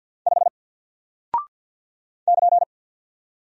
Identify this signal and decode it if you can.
Morse code, heard as a keyed tone, about 50 wpm, 700 Hz. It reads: 5 <KN>
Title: Single Character (Letter:Number:Punctuation:Prosign) Alt 50wpm